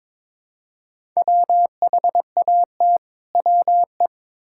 Morse code, heard as a keyed tone, about 21 wpm, 700 Hz. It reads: WHAT WE
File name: Build A Phrase - 2-Word Phrases - 22wpm